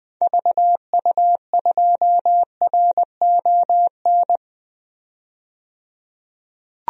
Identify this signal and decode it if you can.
Morse code, heard as a keyed tone, about 21 wpm, 700 Hz. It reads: VU2RON
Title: DXCC-100 Call Signs - 6 Characters - 20wpm